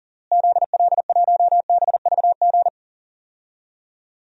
Morse code, heard as a keyed tone, about 40 wpm, 700 Hz. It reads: ZL1BVG
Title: DXCC-100 Call Signs - 6 Characters - Repeated 3x - 40wpm